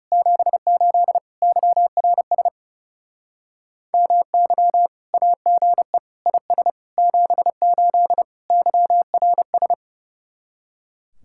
Morse code, heard as a keyed tone, about 36 wpm, 700 Hz. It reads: 78 YRS MY AGE IS 78 YRS
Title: Speed-Racing Common QSO Phrases - 30wpm